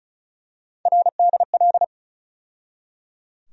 Morse code, heard as a keyed tone, about 34 wpm, 700 Hz. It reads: RDL
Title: USA Call Sign Postfixes - 3 Letters 35wpm